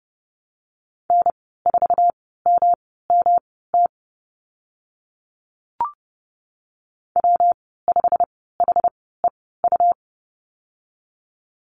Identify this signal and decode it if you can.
Morse code, heard as a keyed tone, about 30 wpm, spaced as for 17 wpm, 700 Hz. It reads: N4MMT W5HEU
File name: Lesson 19g - Letter w - Call Signs 30wpm